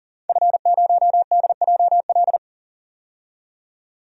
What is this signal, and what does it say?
Morse code, heard as a keyed tone, about 40 wpm, 700 Hz. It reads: F0DJL
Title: DXCC-100 Call Signs - 5 Characters - 40wpm